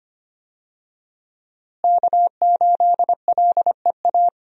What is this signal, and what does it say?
Morse code, heard as a keyed tone, about 25 wpm, 700 Hz. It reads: K8LEA
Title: USA Call Signs - 5 Characters - Repeated 3x - 25wpm